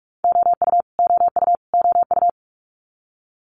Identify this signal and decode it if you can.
Morse code, heard as a keyed tone, about 46 wpm, 700 Hz. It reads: OU OU OU
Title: Multi-letter Phonemes Repeated 3x 45wpm